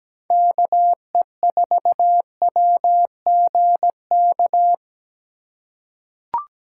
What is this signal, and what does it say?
Morse code, heard as a keyed tone, about 17 wpm, 700 Hz. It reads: KE4WGK
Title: USA Call Signs - 3x Character Spacing - 17wpm